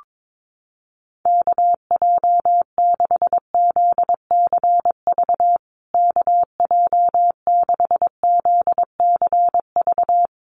Morse code, heard as a keyed tone, about 22 wpm, 700 Hz. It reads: KJ6ZCV KJ6ZCV
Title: USA Call Signs Repeated 3x 22wpm